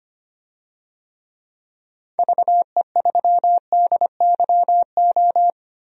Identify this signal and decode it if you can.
Morse code, heard as a keyed tone, about 25 wpm, 700 Hz. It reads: VE3DYO
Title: DXCC-100 Call Signs - 6 Characters - Repeated 3x - 25wpm